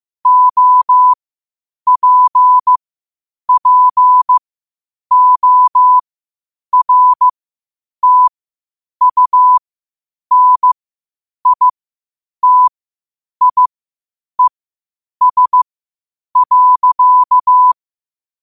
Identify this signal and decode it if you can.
Morse code, heard as a keyed tone, about 15 wpm, spaced as for 8 wpm, 1000 Hz. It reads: OPPORTUNITIES.
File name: qotdfiles / 8wpm